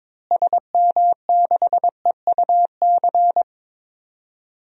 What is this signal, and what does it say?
Morse code, heard as a keyed tone, about 22 wpm, 700 Hz. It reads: SM6EUC
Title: DXCC-100 Call Signs - 6 Characters - Repeated 2x - 22wpm